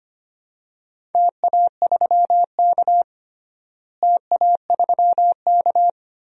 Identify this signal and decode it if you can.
Morse code, heard as a keyed tone, about 25 wpm, 700 Hz. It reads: TA3K TA3K
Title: DXCC-100 Call Signs - 4 Characters - Repeated 3x - 25wpm